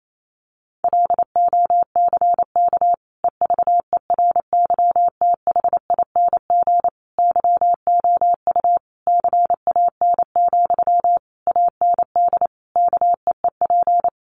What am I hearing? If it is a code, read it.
Morse code, heard as a keyed tone, about 28 wpm, 700 Hz. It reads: LOCK EVERYTHING YOU CAN, AND KEEP
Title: Sentences from Top 1300 Words 28wpm